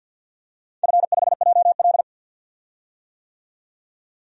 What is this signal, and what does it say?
Morse code, heard as a keyed tone, about 47 wpm, 700 Hz. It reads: F5JL
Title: DXCC-100 Call Signs - 4 Characters - 50wpm